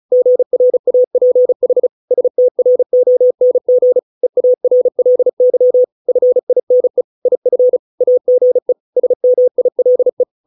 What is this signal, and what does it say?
Morse code, heard as a keyed tone, about 36 wpm, 500 Hz. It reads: GRAPH STRONG EARLY FINE IF AGE SMILE